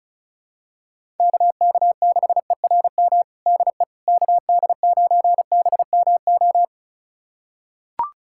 Morse code, heard as a keyed tone, about 36 wpm, 700 Hz. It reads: KK6ERM DE KD9BMO
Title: USA Callsign DE Callsign - 6 Characters - 35wpm